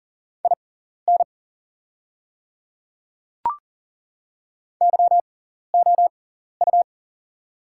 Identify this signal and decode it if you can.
Morse code, heard as a keyed tone, about 40 wpm, spaced as for 14 wpm, 700 Hz. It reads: IN YOU
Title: 6x Character Spacing - Top 100 Words - 40wpm